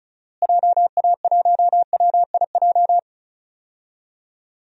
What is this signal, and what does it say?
Morse code, heard as a keyed tone, about 36 wpm, 700 Hz. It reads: JA1WIJ
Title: DXCC-100 Call Signs - 6 Characters - Repeated 2x - 35wpm